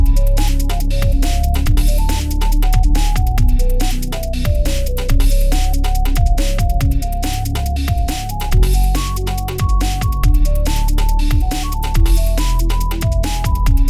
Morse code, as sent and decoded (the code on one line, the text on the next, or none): none